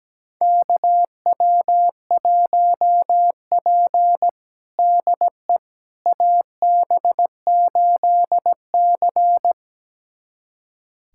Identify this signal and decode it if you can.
Morse code, heard as a keyed tone, about 17 wpm, 700 Hz. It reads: KW1P DE AB8C
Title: USA Callsign DE Callsign - 4 Characters - 17wpm